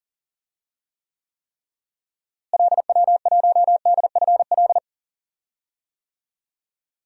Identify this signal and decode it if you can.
Morse code, heard as a keyed tone, about 40 wpm, 700 Hz. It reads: LW1DFL
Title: DXCC-100 Call Signs - 6 Characters - Repeated 3x - 40wpm